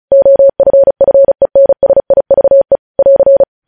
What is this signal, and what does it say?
Morse code, heard as a keyed tone, about 36 wpm, 550 Hz. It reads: OFFENSIVE <AR>
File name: podcast / News headlines-1x-35x35